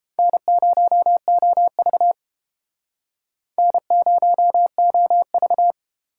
Morse code, heard as a keyed tone, about 30 wpm, 700 Hz. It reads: N0OV N0OV